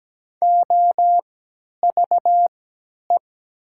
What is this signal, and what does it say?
Morse code, heard as a keyed tone, about 17 wpm, spaced as for 10 wpm, 700 Hz. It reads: OVE
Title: 3x Character Spacing - Top 500 Words - 17wpm